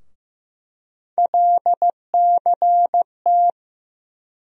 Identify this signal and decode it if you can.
Morse code, heard as a keyed tone, about 15 wpm, 700 Hz. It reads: LCT
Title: USA Call Sign Postfixes - 3 Letters 15wpm